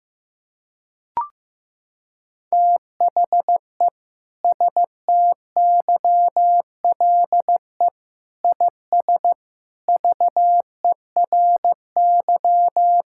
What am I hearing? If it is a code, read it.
Morse code, heard as a keyed tone, about 15 wpm, 700 Hz. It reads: THE STYLE IS VERY
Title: Sentences from Top 1000 Words - Review Part 5 - 15wpm